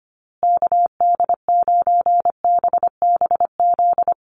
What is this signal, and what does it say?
Morse code, heard as a keyed tone, about 25 wpm, 700 Hz. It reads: KD9BBZ
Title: USA Call Signs Repeated 3x 25wpm